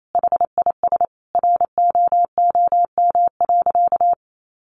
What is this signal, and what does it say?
Morse code, heard as a keyed tone, about 28 wpm, 700 Hz. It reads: HIS ROOM.
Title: Sentences from Top 300 Words 28wpm